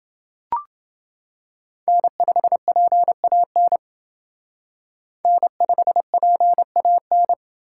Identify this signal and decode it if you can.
Morse code, heard as a keyed tone, about 29 wpm, 700 Hz. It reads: N5PAN N5PAN